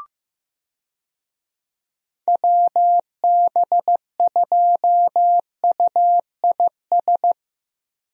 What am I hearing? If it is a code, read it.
Morse code, heard as a keyed tone, about 16 wpm, 700 Hz. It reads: WB2UIS
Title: USA Call Signs - 6 Characters - Repeated 2x - 15wpm